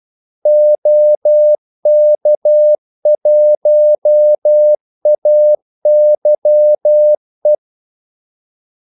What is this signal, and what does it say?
Morse code, heard as a keyed tone, about 12 wpm, 600 Hz. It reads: OK1AYE
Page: 2 ekstra ordmellemrum mellem kaldesignalerne